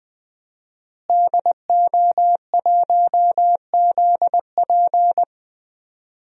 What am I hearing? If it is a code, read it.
Morse code, heard as a keyed tone, about 20 wpm, 700 Hz. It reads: DO1ZP
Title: DXCC-100 Call Signs - 5 Characters - Repeated 3x - 20wpm